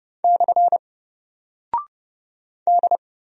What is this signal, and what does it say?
Morse code, heard as a keyed tone, about 30 wpm, 700 Hz. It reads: / D
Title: Single Character (Letter:Number:Punctuation:Prosign) 30wpm